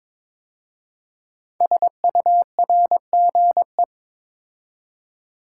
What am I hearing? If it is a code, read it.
Morse code, heard as a keyed tone, about 23 wpm, 700 Hz. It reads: SURGE
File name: 5 Letter Words - First 3 Letters Given - 22wpm